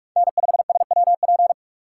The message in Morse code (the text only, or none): N5SWP